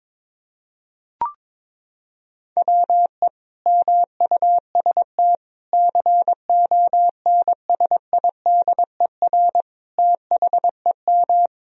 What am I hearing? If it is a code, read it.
Morse code, heard as a keyed tone, about 22 wpm, 700 Hz. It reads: WE MUST CONSIDER THEM
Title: Sentences from Top 1000 Words - Review Part 4 - 22wpm